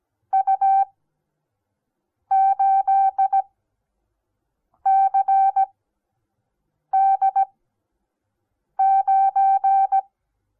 Morse code, heard as a keyed tone, about 17 wpm, spaced as for 5 wpm, 780 Hz. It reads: U8CD9
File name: Onlinelehrgang / morsen / DJ4UF Morsen08